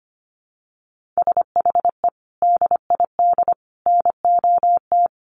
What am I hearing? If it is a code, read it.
Morse code, heard as a keyed tone, about 25 wpm, 700 Hz. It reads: SHE DID NOT
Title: Sets of 3 Words 25wpm